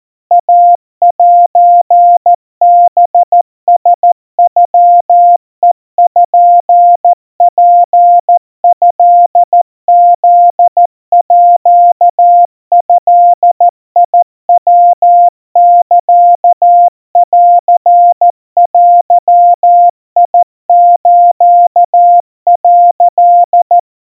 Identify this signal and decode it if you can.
Morse code (Japanese Wabun code), heard as a keyed tone, about 14 wpm, 700 Hz. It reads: イセハラノペツトフードヤサンデス」
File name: kawa / Morse sample / kamyu 50 cpm 700 Hz Crispy